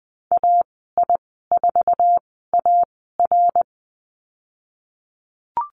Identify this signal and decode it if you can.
Morse code, heard as a keyed tone, about 20 wpm, 700 Hz. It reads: A I 4 A R
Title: 082 Lesson 14g - Number 2 - Call Signs 20wpm